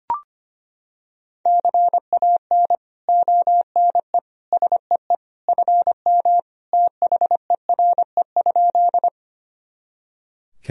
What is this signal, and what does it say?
Morse code, heard as a keyed tone, about 25 wpm, 700 Hz. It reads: CAN ONE SEE FM THERE?
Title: Sentences from Top 100 Words and Abbreviations 25wpm